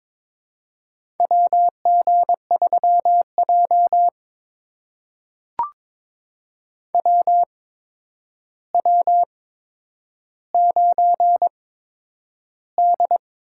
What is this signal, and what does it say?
Morse code, heard as a keyed tone, about 23 wpm, spaced as for 10 wpm, 700 Hz. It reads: WG3J WW9D
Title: USA Call Signs - 4 Characters - 8x Character Spacing - 22wpm